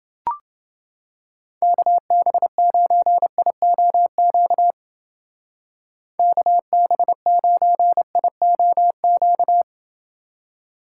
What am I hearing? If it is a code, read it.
Morse code, heard as a keyed tone, about 29 wpm, 700 Hz. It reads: KB9IOQ KB9IOQ